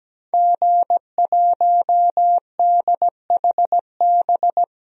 Morse code, heard as a keyed tone, about 17 wpm, 700 Hz. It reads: G1DHB
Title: DXCC-100 Call Signs - 5 Characters - Repeated 3x - 17wpm